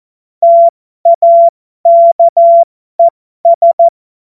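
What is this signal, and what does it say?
Morse code, heard as a keyed tone, about 14 wpm, spaced as for 12 wpm, 680 Hz. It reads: TAKES
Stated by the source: Morse joke practice at effective 60, Farnsworth 70.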